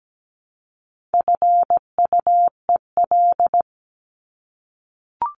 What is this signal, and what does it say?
Morse code, heard as a keyed tone, about 17 wpm, 700 Hz. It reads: FUEL
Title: Top 1751-2000 Words 17wpm